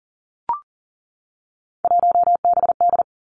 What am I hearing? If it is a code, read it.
Morse code, heard as a keyed tone, about 38 wpm, 700 Hz. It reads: 1BD
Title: Three Challenging Letter-Number Combinations 40wpm